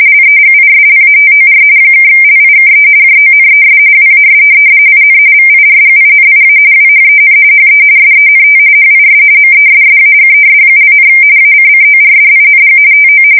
RTTY, about 45.45 baud. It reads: LEGISLATOR (LIKE JONES WAS IN VIRGINIA WHEN HE SENT HIS MESSAGES), ACCORDING TO
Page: 170 Shift 2125/2295